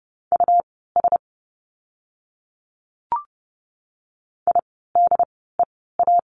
Morse code, heard as a keyed tone, about 30 wpm, spaced as for 17 wpm, 700 Hz. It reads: US IDEA
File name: course / 30:10wpm / 108 Lesson 18f - Letter m - Words 30wpm